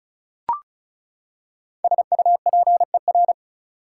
Morse code, heard as a keyed tone, about 34 wpm, 700 Hz. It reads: SUPER
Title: Prefixes and Suffixes 35wpm